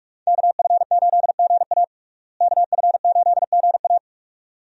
Morse code, heard as a keyed tone, about 46 wpm, 700 Hz. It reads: KF8GA KF8GA
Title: USA Call Signs - 5 Characters - Repeated 2x - 45wpm